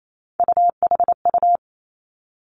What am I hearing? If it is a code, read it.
Morse code, heard as a keyed tone, about 28 wpm, 700 Hz. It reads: UHU
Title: Three Challenging Letter-Number Combinations 28wpm